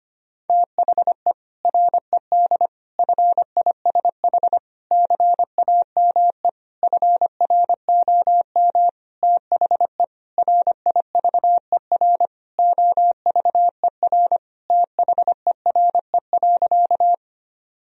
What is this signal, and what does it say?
Morse code, heard as a keyed tone, about 25 wpm, 700 Hz. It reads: THE RED FISH CAME FROM THE RIVER OVER THERE.
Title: Sentences from Top 1000 Words - Review Part 2 - 25wpm